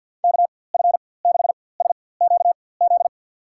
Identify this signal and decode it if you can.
Morse code, heard as a keyed tone, about 53 wpm, spaced as for 24 wpm, 700 Hz. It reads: KF6SQZ
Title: USA Call Signs - 6 Characters - 4x Character Spacing - 50wpm